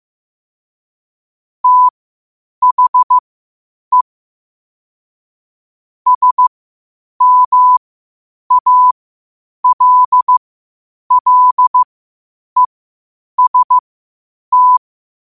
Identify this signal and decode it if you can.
Morse code, heard as a keyed tone, about 15 wpm, spaced as for 8 wpm, 1000 Hz. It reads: THE SMALLEST
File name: qotdfiles / 8wpm